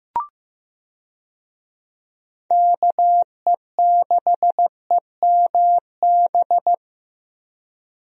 Morse code, heard as a keyed tone, about 15 wpm, 700 Hz. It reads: KE6EMB
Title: USA Call Signs - 6 Characters - Repeated 2x - 15wpm